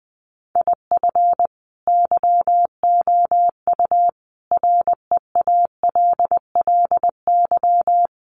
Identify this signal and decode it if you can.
Morse code, heard as a keyed tone, about 19 wpm, 700 Hz. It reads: IF YOU REALLY
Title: Sets of 5 Words Encore2 20wpm